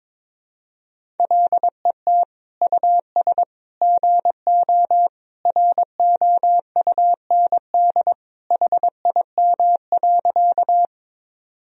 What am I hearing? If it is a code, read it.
Morse code, heard as a keyed tone, about 22 wpm, 700 Hz. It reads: LET US GO ROUND HIM.
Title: Sentences from Top 1000 Words - Review Part 1 - 22wpm